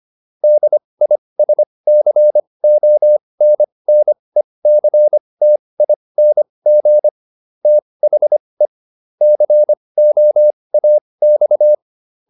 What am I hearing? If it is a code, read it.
Morse code, heard as a keyed tone, about 25 wpm, spaced as for 20 wpm, 600 Hz. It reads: DISCONNECTING THE COAX